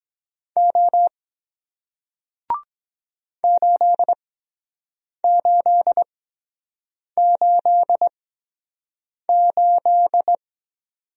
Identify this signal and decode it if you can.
Morse code, heard as a keyed tone, about 23 wpm, spaced as for 7 wpm, 700 Hz. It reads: O 8888